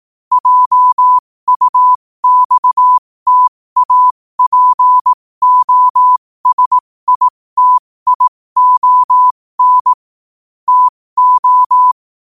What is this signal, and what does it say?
Morse code, heard as a keyed tone, about 18 wpm, spaced as for 15 wpm, 1000 Hz. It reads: JUXTAPOSITION TO
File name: qotdfiles / 15wpm